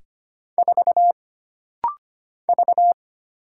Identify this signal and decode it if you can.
Morse code, heard as a keyed tone, about 25 wpm, 700 Hz. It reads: 4 V
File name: Single Character (Letter:Number:Punctuation:Prosign) - Rapid-Fire - 25wpm